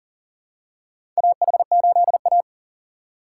Morse code, heard as a keyed tone, about 40 wpm, 700 Hz. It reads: AH8A
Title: USA Call Signs - 2x Character Spacing - 40wpm